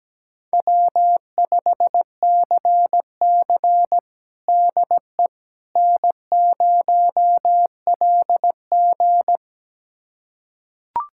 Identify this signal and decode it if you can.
Morse code, heard as a keyed tone, about 17 wpm, 700 Hz. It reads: W5CC DE N0LG
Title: USA Callsign DE Callsign - 4 Characters - 17wpm